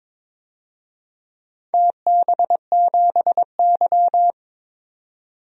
Morse code, heard as a keyed tone, about 22 wpm, 700 Hz. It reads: TB7Y
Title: DXCC-100 Call Signs - 4 Characters - 22wpm